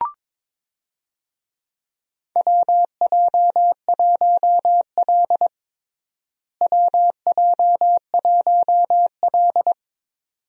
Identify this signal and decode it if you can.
Morse code, heard as a keyed tone, about 22 wpm, 700 Hz. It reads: WJ1L WJ1L
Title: USA Call Signs - 4 Characters - Repeated 2x - 22wpm